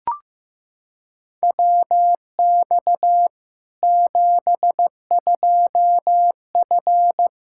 Morse code, heard as a keyed tone, about 15 wpm, 700 Hz. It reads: WX 72F
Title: Practice QSO - Exchange 2 - Info - 15wpm